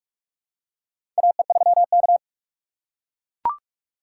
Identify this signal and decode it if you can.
Morse code, heard as a keyed tone, about 41 wpm, 700 Hz. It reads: AE3K
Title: USA Call Signs - 4 Characters - 3x Character Spacing - 45wpm